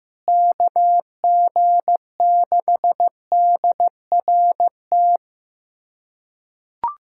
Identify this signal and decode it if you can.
Morse code, heard as a keyed tone, about 16 wpm, 700 Hz. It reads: KG6DRT
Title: USA Callsign DE Callsign - 6 Characters - 15wpm